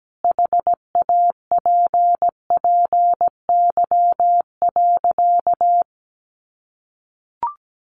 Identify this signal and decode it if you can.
Morse code, heard as a keyed tone, about 17 wpm, 700 Hz. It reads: HAPPY.
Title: Sentences from Top 1000 Words 17wpm